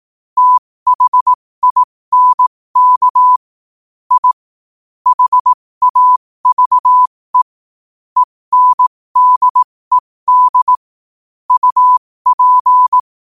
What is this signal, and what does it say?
Morse code, heard as a keyed tone, about 18 wpm, spaced as for 15 wpm, 1000 Hz. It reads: THINK I HAVE ENDED UP
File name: qotdfiles / 15wpm